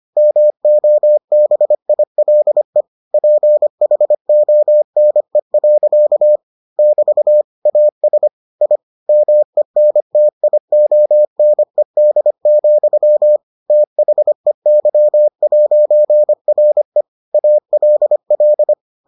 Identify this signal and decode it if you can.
Morse code, heard as a keyed tone, about 25 wpm, 600 Hz. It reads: MOBILE PHONE. <BT> AS I MENTIONED, THEY'RE ALL